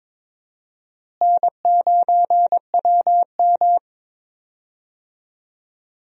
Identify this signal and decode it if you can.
Morse code, heard as a keyed tone, about 22 wpm, 700 Hz. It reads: N9WM
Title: USA Call Signs - 4 Characters - Repeated 2x - 22wpm